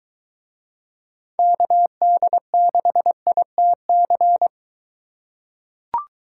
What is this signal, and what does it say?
Morse code, heard as a keyed tone, about 21 wpm, 700 Hz. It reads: KD6ITC